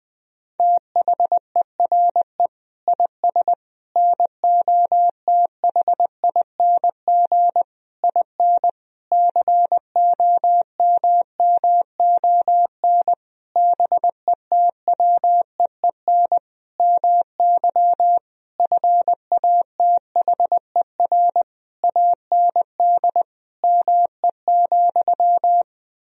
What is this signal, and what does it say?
Morse code, heard as a keyed tone, about 20 wpm, 700 Hz. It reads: THERE IS NOTHING IN COMMON BETWEEN MY FATHER AND ME,
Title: Sentences from Top 1000 Words - Review Part 3 - 20wpm